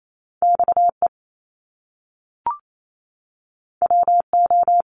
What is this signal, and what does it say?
Morse code, heard as a keyed tone, about 28 wpm, 700 Hz. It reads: XE WO
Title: Top Two Letter Combinations in Words 28wpm